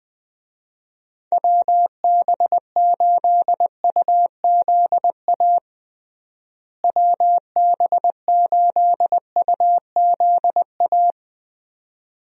Morse code, heard as a keyed tone, about 20 wpm, 700 Hz. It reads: WB8UZA WB8UZA
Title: USA Call Signs - 6 Characters - Repeated 2x - 20wpm